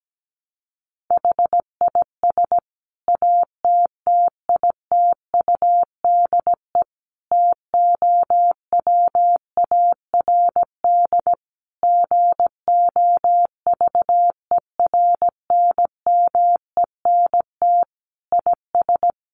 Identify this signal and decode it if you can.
Morse code, heard as a keyed tone, about 17 wpm, 700 Hz. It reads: HIS ATTITUDE TOWARD GOVERNMENT IS
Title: Sentences from Top 1300 Words 17wpm